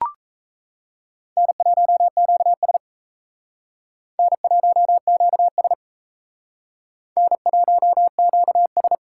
Morse code, heard as a keyed tone, about 36 wpm, 700 Hz. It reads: N1QS N1QS N1QS